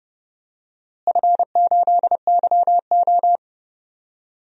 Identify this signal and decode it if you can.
Morse code, heard as a keyed tone, about 30 wpm, 700 Hz. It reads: F8YO
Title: DXCC-100 Call Signs - 4 Characters - Repeated 2x - 30wpm